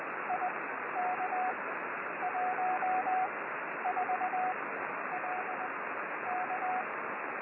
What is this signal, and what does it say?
Morse code, heard as a keyed tone, about 19 wpm, spaced as for 10 wpm, 740 Hz, with radio noise. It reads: IK14LK
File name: 6V7SIX B